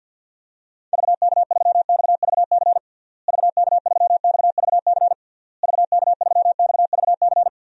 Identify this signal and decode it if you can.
Morse code, heard as a keyed tone, about 51 wpm, 700 Hz. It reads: VK3XVC VK3XVC VK3XVC
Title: DXCC-100 Call Signs - 6 Characters - Repeated 3x - 50wpm